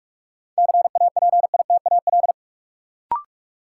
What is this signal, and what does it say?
Morse code, heard as a keyed tone, about 46 wpm, 700 Hz. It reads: CAPITAL
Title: Sentences from Top 1000 Words - New Word Drill - 45wpm